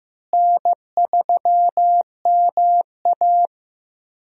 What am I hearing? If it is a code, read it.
Morse code, heard as a keyed tone, about 15 wpm, 700 Hz. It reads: N3MA
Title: USA Call Signs - 4 Characters - Repeated 3x - 15wpm